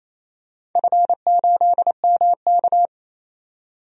Morse code, heard as a keyed tone, about 29 wpm, 700 Hz. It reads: F8MK
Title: DXCC-100 Call Signs - 4 Characters - 28wpm